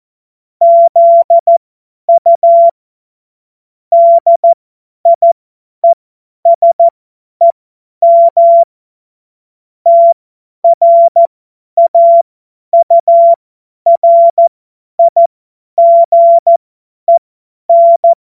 Morse code, heard as a keyed tone, about 14 wpm, spaced as for 10 wpm, 680 Hz. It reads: ZU DIESEM TRAURIGEN
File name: Morsekurs / Witze / Witz 27. Januar 2026 50 F70